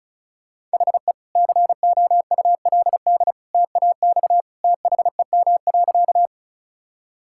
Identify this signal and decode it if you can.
Morse code, heard as a keyed tone, about 34 wpm, 700 Hz. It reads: HE COULD TAX THEM.
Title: Sentences from Top 1000 Words - Review Part 5 - 35wpm